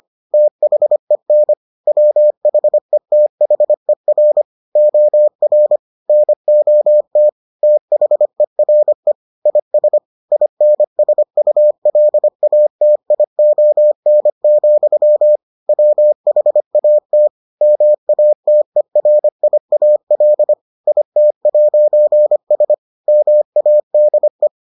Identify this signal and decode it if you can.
Morse code, heard as a keyed tone, about 25 wpm, 600 Hz. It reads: THEN WHETHER OR NOT THERE IS INSULATION, WHAT MATERIAL IT'S MADE